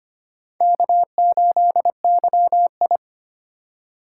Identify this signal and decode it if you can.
Morse code, heard as a keyed tone, about 25 wpm, 700 Hz. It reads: K8YI
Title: USA Call Signs - 4 Characters - 4x Character Spacing - 25wpm